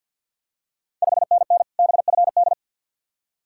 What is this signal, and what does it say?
Morse code, heard as a keyed tone, about 46 wpm, 690 Hz. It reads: 5NN BFD
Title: Ontario QSO Party Exchange 50wpm